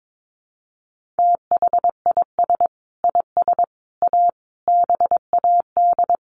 Morse code, heard as a keyed tone, about 22 wpm, 700 Hz. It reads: THIS IS A BAD
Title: Sentences from Top 1400 Words 22wpm